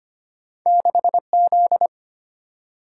Morse code, heard as a keyed tone, about 25 wpm, 690 Hz. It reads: 6Z
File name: Two Letter-Number Combinations - 25wpm